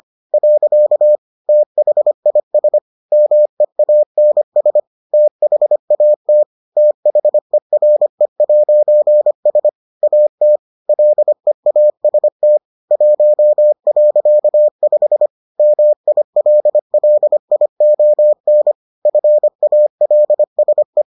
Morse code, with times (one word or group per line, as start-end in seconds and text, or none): 0.3-1.2 .
1.5-2.8 THIS
3.1-4.8 MEANS
5.1-6.4 THAT
6.8-9.7 THERE'S
10.0-10.6 AT
10.9-12.6 LEAST
12.9-15.3 1.5
15.6-18.7 MILLION
19.1-21.0 FALSE